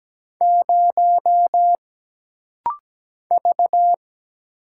Morse code, heard as a keyed tone, about 17 wpm, 700 Hz. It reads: 0 V
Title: Single Letter-Number Alt - Rapid-Fire 17wpm